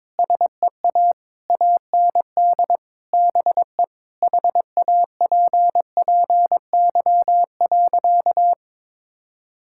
Morse code, heard as a keyed tone, about 22 wpm, 700 Hz. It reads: SEA AND BE HAPPY.
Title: Sentences from Top 1000 Words - Review Part 5 - 22wpm